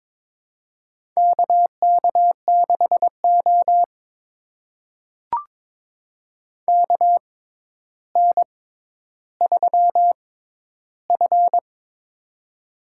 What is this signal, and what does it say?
Morse code, heard as a keyed tone, about 22 wpm, spaced as for 11 wpm, 700 Hz. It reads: KK6O KN3F
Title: USA Call Signs - 4 Characters - 6x Character Spacing - 22wpm